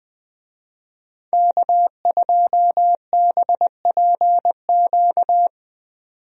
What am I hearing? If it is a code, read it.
Morse code, heard as a keyed tone, about 20 wpm, 700 Hz. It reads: K2BPQ
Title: USA Call Signs - 5 Characters - 20wpm